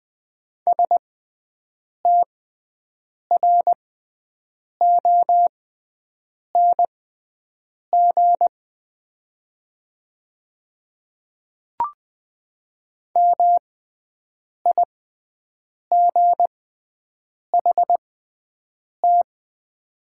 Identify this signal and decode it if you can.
Morse code, heard as a keyed tone, about 20 wpm, spaced as for 7 wpm, 700 Hz. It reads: STRONG MIGHT
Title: 6x Character Spacing - Top 500 Words - 20wpm